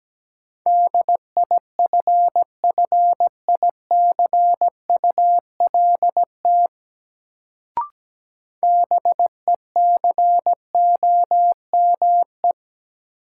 Morse code, heard as a keyed tone, about 17 wpm, 700 Hz. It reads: DIFFICULT BECOME
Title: Sentences from Top 600 Words - New Word Drill - 17wpm